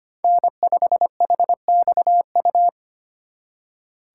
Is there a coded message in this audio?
Yes: Morse code from 0.2 to 2.7 s, heard as a keyed tone, about 32 wpm, 700 Hz. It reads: N5HXU